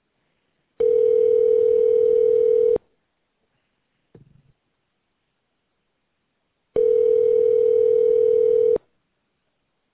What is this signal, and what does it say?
Ringback tone (North America), 440 + 480 Hz.